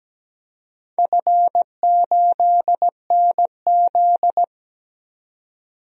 Morse code, heard as a keyed tone, about 17 wpm, 700 Hz. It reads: F8NZ
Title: DXCC-100 Call Signs - 4 Characters - Repeated 2x - 17wpm